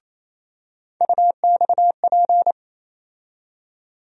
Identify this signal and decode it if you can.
Morse code, heard as a keyed tone, about 28 wpm, 700 Hz. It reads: UXP
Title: USA Call Sign Postfixes - 3 Letters 28wpm